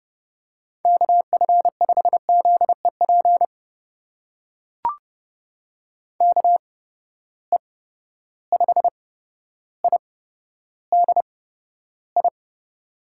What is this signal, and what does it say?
Morse code, heard as a keyed tone, about 30 wpm, spaced as for 13 wpm, 700 Hz. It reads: KF5ZEP KE5IDI
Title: USA Call Signs - 6 Characters - 8x Character Spacing - 30wpm